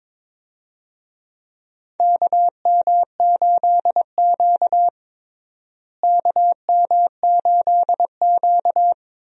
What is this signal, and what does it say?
Morse code, heard as a keyed tone, about 22 wpm, 700 Hz. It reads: KM8Q KM8Q
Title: USA Call Signs - 4 Characters - Repeated 2x - 22wpm